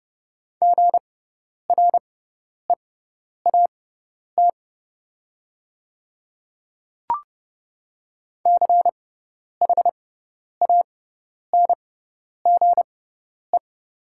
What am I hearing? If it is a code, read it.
Morse code, heard as a keyed tone, about 30 wpm, spaced as for 10 wpm, 700 Hz. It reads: GREAT CHANGE
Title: 6x Character Spacing - Top 200 Words - 30wpm